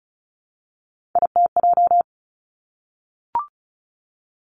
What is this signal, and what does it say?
Morse code, heard as a keyed tone, about 36 wpm, 700 Hz. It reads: ITJ
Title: Three Challenging Letter-Number Combinations 35wpm